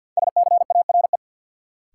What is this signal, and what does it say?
Morse code, heard as a keyed tone, about 52 wpm, 700 Hz. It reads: SCARE
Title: Sentences from Top 1000 Words - Review Part 5 - New Word Drill - 50wpm